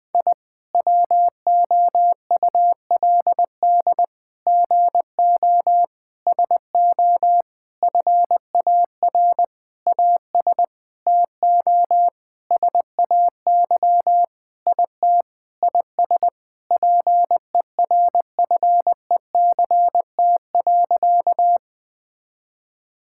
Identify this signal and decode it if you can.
Morse code, heard as a keyed tone, about 21 wpm, 700 Hz. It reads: I WOULD GO SO FAR AS TO SAY IT IS PERFECT.
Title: Sentences with 8-Word Phrases - Spoken Beginning - 20wpm